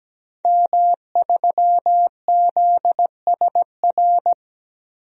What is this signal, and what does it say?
Morse code, heard as a keyed tone, about 17 wpm, 700 Hz. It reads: M3ZSR
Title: DXCC-100 Call Signs - 5 Characters - 17wpm